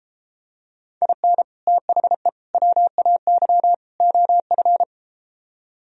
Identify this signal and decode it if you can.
Morse code, heard as a keyed tone, about 34 wpm, 700 Hz. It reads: IN THE WAY OF